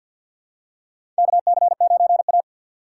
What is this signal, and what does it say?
Morse code, heard as a keyed tone, about 51 wpm, 700 Hz. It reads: KC9A
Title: USA Call Signs - 4 Characters - 6x Character Spacing - 50wpm